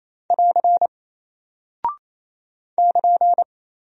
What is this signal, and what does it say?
Morse code, heard as a keyed tone, about 28 wpm, 700 Hz. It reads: <AR> <KN>
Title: Single Character (Letter:Number:Punctuation:Prosign) 28wpm